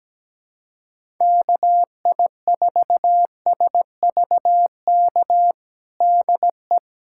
Morse code, heard as a keyed tone, about 17 wpm, 700 Hz. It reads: KI4SVK DE
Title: USA Callsign DE Callsign - 6 Characters - 17wpm